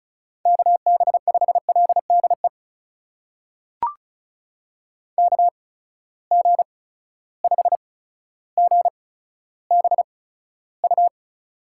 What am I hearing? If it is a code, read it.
Morse code, heard as a keyed tone, about 36 wpm, spaced as for 15 wpm, 700 Hz. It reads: KB5LDE KG5GBU
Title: USA Call Signs - 8x Character Spacing - 35wpm